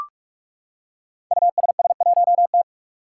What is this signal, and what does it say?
Morse code, heard as a keyed tone, about 44 wpm, 700 Hz. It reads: USS1T
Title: Five Challenging Letter-Number Combinations 45wpm